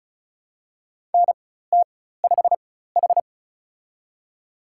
Morse code, heard as a keyed tone, about 36 wpm, spaced as for 16 wpm, 700 Hz. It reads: NT5H
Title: USA Call Signs - 4 Characters - 4x Character Spacing - 35wpm